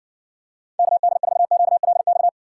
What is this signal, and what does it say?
Morse code, heard as a keyed tone, about 60 wpm, 700 Hz. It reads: KD4QLX